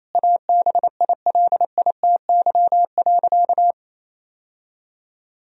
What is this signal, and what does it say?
Morse code, heard as a keyed tone, about 28 wpm, 700 Hz. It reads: ABILITY.
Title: Sentences from Top 1000 Words - Review Part 3 - 28wpm